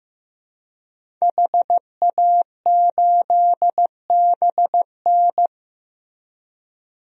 Morse code, heard as a keyed tone, about 16 wpm, 700 Hz. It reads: HA8BN
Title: DXCC-100 Call Signs - 5 Characters - Repeated 2x - 15wpm